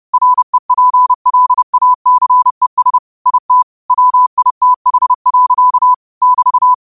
Morse code, heard as a keyed tone, about 30 wpm, 1000 Hz. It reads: REPLACES IT WITH. <BT>
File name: qotdfiles / 30wpm